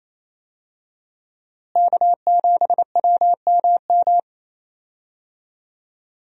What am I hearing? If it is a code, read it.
Morse code, heard as a keyed tone, about 28 wpm, 700 Hz. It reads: K7WMM
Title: USA Call Signs - 5 Characters - Repeated 3x - 28wpm